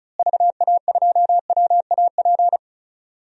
Morse code, heard as a keyed tone, about 36 wpm, 700 Hz. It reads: VA2WAP
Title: DXCC-100 Call Signs - 6 Characters - Repeated 2x - 35wpm